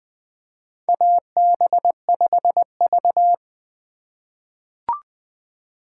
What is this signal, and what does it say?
Morse code, heard as a keyed tone, about 20 wpm, 700 Hz. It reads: AB5V
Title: USA Call Signs - 4 Characters - 2x Character Spacing - 20wpm